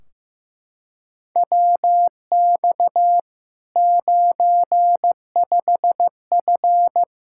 Morse code, heard as a keyed tone, about 15 wpm, 700 Hz. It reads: WX 95F
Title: K1USN SST QSO Practice 15wpm